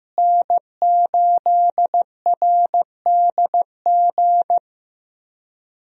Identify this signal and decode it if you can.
Morse code, heard as a keyed tone, about 15 wpm, 700 Hz. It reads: N8RDG
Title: USA Call Signs - 5 Characters - Repeated 2x - 15wpm